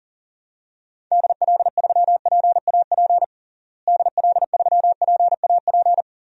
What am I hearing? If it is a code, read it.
Morse code, heard as a keyed tone, about 40 wpm, 700 Hz. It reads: DL3PAP DL3PAP
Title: DXCC-100 Call Signs - 6 Characters - Repeated 2x - 40wpm